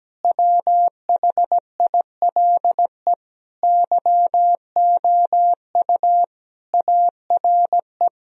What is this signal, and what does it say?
Morse code, heard as a keyed tone, about 17 wpm, 700 Hz. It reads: WHILE YOU ARE
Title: Sentences from Top 1000 Words - Review Part 1 - 17wpm